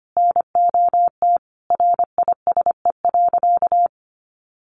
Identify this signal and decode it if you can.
Morse code, heard as a keyed tone, about 25 wpm, 700 Hz. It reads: NOT RISE.
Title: Sentences from Top 900 Words 25wpm